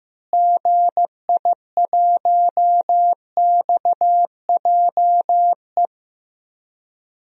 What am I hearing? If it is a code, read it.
Morse code, heard as a keyed tone, about 15 wpm, 700 Hz. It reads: GI1XJE
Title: DXCC-100 Call Signs - 6 Characters - Repeated 2x - 15wpm